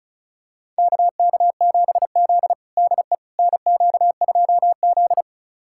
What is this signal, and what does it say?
Morse code, heard as a keyed tone, about 34 wpm, 700 Hz. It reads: KK7Z DE NQ2Z
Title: USA Callsign DE Callsign - 4 Characters - 35wpm